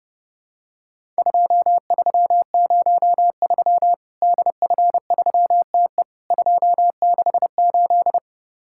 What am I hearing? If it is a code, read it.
Morse code, heard as a keyed tone, about 30 wpm, 700 Hz. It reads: 2303 DF3TE 268
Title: Worked All Europe DX Contest - QTC Exchange - 30wpm